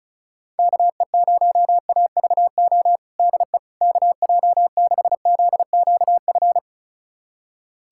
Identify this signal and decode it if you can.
Morse code, heard as a keyed tone, about 34 wpm, 700 Hz. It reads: KE0AVO DE KJ6ZQF
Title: USA Callsign DE Callsign 35wpm